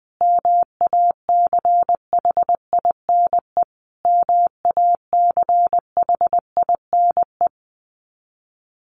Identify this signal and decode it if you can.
Morse code, heard as a keyed tone, about 20 wpm, 700 Hz. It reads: MACHINE MACHINE
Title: Top 500 Words Repeated 20wpm